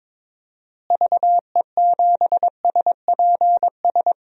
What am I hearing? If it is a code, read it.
Morse code, heard as a keyed tone, about 22 wpm, 700 Hz. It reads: VE7SPS
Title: DXCC-100 Call Signs - 6 Characters - Repeated 2x - 22wpm